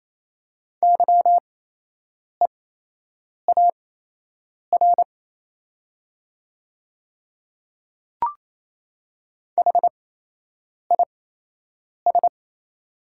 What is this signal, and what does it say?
Morse code, heard as a keyed tone, about 28 wpm, spaced as for 8 wpm, 700 Hz. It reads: YEAR HIS
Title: 8x Character Spacing - Top 300 Words - 28wpm